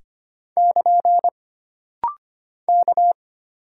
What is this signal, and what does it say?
Morse code, heard as a keyed tone, about 25 wpm, 700 Hz. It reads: <KN> K
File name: Single Character (Letter:Number:Punctuation:Prosign) - Rapid-Fire - 25wpm